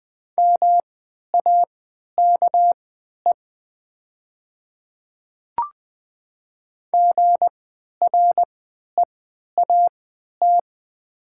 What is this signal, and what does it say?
Morse code, heard as a keyed tone, about 20 wpm, spaced as for 11 wpm, 700 Hz. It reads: MAKE GREAT
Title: 3x Character Spacing - Top 300 Words - 20wpm